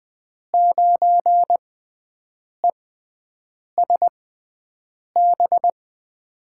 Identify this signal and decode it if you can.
Morse code, heard as a keyed tone, about 20 wpm, spaced as for 7 wpm, 700 Hz. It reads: 9ESB
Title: USA Call Signs - 5 Characters - 6x Character Spacing - 20wpm